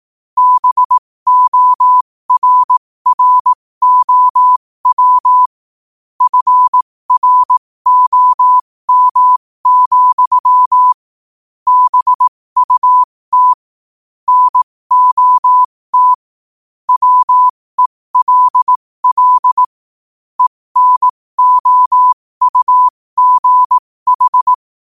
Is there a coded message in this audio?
Yes: Morse code from 0.4 to 24.5 s, heard as a keyed tone, about 18 wpm, spaced as for 15 wpm, 1000 Hz. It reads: BORROW FROM, BUT NOT WELL ENOUGH